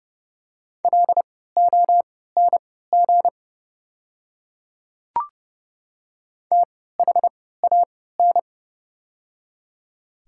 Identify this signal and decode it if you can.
Morse code, heard as a keyed tone, about 30 wpm, spaced as for 17 wpm, 700 Hz. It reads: LONG THAN
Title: 3x Character Spacing - Top 100 Words - 30wpm